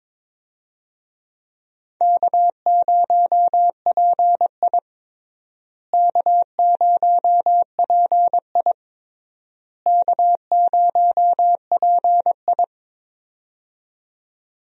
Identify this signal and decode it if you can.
Morse code, heard as a keyed tone, about 22 wpm, 700 Hz. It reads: K0PI K0PI K0PI
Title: USA Call Signs - 4 Characters - Repeated 3x - 22wpm